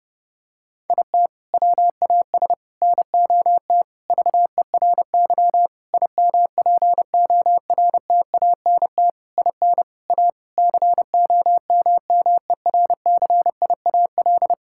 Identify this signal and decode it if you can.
Morse code, heard as a keyed tone, about 30 wpm, 700 Hz. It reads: IT WAS NOT VERY IMPORTANT IN A COMMERCIAL
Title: Sentences from Top 1000 Words - Review Part 5 - 30wpm